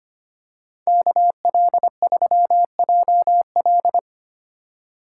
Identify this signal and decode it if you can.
Morse code, heard as a keyed tone, about 28 wpm, 700 Hz. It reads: KL3JL
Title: Speed Racing - USA 2x2 Call Signs 25wpm